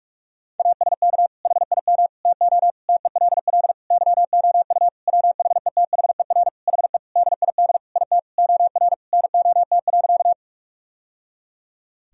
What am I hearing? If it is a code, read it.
Morse code, heard as a keyed tone, about 41 wpm, 700 Hz. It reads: ASK HIM TO TELL YOU WHETHER HE DID IT OR NOT.
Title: Sentences from Top 1000 Words - Review Part 3 - 45wpm